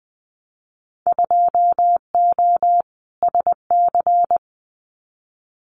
Morse code, heard as a keyed tone, about 20 wpm, 700 Hz. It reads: 2O SC
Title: Winter Field Day Exchanges Encore 20wpm